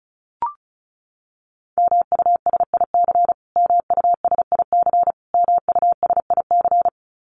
Morse code, heard as a keyed tone, about 34 wpm, 700 Hz. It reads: MUSIC MUSIC MUSIC
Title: Top 500 Words Repeated 35wpm